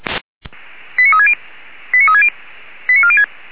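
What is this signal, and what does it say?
Five-tone selective call (ZVEI), heard as quick runs of five tones, tones of about 70 ms. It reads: 89360 89360 89406
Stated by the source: ZVEI five-tone selective call received by SDR, code 89384.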